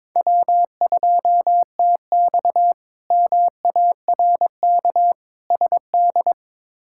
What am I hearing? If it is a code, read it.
Morse code, heard as a keyed tone, about 21 wpm, 700 Hz. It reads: W2TX MARK SD
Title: K1USN SST QSO Practice - 22wpm